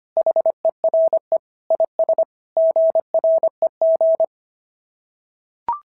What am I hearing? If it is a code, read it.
Morse code, heard as a keyed tone, about 24 wpm, 650 Hz. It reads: HERE IS GREG
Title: SOTA QSO Practice 25wpm